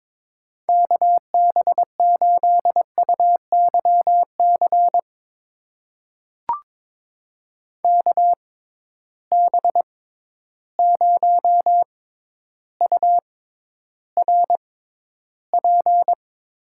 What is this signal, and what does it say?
Morse code, heard as a keyed tone, about 21 wpm, spaced as for 11 wpm, 700 Hz. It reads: KB8UYC KB0URP
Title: USA Call Signs - 6 Characters - 6x Character Spacing - 22wpm